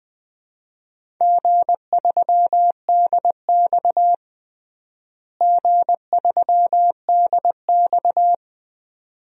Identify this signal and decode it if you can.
Morse code, heard as a keyed tone, about 20 wpm, 700 Hz. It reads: G3DX G3DX
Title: DXCC-100 Call Signs - 4 Characters - Repeated 2x - 20wpm